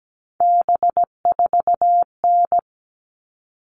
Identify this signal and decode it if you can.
Morse code, heard as a keyed tone, about 17 wpm, 700 Hz. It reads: B4N
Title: All Cut Numbers - 3-Digit 17wpm